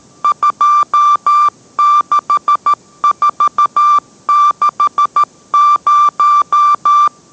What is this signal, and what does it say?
Morse code, heard as a keyed tone, about 13 wpm, 1180 Hz. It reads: 26460